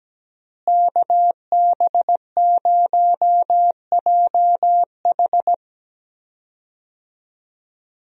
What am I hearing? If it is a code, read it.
Morse code, heard as a keyed tone, about 17 wpm, 700 Hz. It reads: KB0JH
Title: USA Call Signs - 5 Characters - Repeated 3x - 17wpm